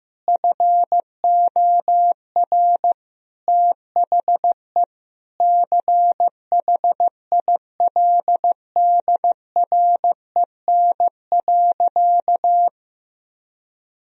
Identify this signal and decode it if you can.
Morse code, heard as a keyed tone, about 15 wpm, 700 Hz. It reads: FOR THE CHILDREN.
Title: Sentences from Top 1000 Words - Review Part 2 - 15wpm